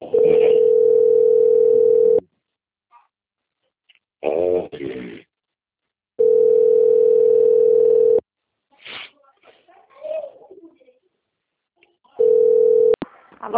Ringback tone (North America), 440 + 480 Hz.
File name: out- - - - - .30580